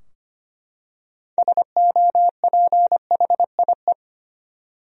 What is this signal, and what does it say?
Morse code, heard as a keyed tone, about 25 wpm, 700 Hz. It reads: SOPHIE
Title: USA Common Names - 25wpm